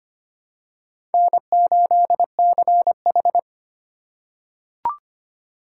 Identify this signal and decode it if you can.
Morse code, heard as a keyed tone, about 25 wpm, 700 Hz. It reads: N8CH
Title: USA Call Signs - 4 Characters - 4x Character Spacing - 25wpm